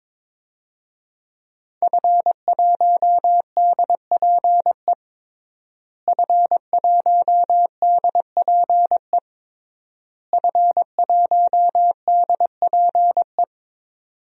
Morse code, heard as a keyed tone, about 22 wpm, 700 Hz. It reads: F1DPE F1DPE F1DPE
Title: DXCC-100 Call Signs - 5 Characters - Repeated 3x - 22wpm